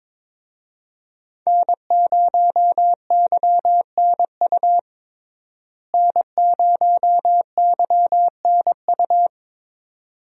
Morse code, heard as a keyed tone, about 22 wpm, 700 Hz. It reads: N0YNU N0YNU
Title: USA Call Signs - 5 Characters - Repeated 2x - 22wpm